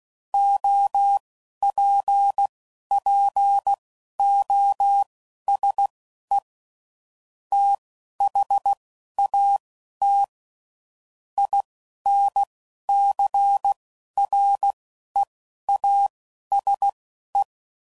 Morse code, heard as a keyed tone, about 16 wpm, spaced as for 11 wpm, 800 Hz. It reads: OPPOSE THAT INCREASE
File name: CW Code Groups @10WPM / Technical Subjects / XC